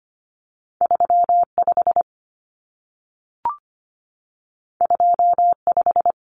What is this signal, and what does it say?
Morse code, heard as a keyed tone, about 25 wpm, 700 Hz. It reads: 35 25
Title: Numbers - 2-Digit 25wpm